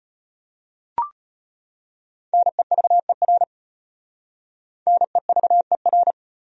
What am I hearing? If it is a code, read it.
Morse code, heard as a keyed tone, about 35 wpm, 700 Hz. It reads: NEVER NEVER